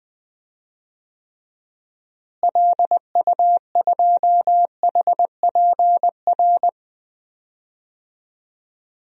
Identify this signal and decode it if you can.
Morse code, heard as a keyed tone, about 20 wpm, 700 Hz. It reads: LU2HPR
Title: DXCC-100 Call Signs - 6 Characters - Repeated 2x - 20wpm